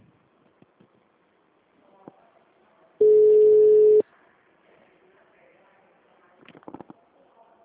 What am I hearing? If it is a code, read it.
Ringback tone (Europe, 425 Hz).
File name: out- -2015- - - .1938